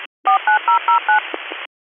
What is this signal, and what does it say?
DTMF, heard as telephone touch tones, tones of about 120 ms. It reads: 49009
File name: 101- .0-call 2782